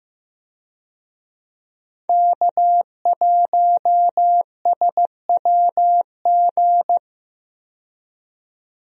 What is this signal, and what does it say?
Morse code, heard as a keyed tone, about 15 wpm, 700 Hz. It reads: K1SWG
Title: USA Call Signs - 5 Characters - 15wpm